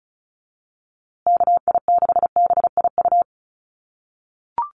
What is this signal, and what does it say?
Morse code, heard as a keyed tone, about 34 wpm, 700 Hz. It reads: KI6BIU
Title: USA Call Signs Repeated 3x 35wpm